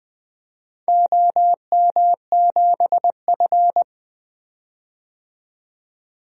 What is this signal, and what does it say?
Morse code, heard as a keyed tone, about 20 wpm, 700 Hz. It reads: OM7F
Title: DXCC-100 Call Signs - 4 Characters - 20wpm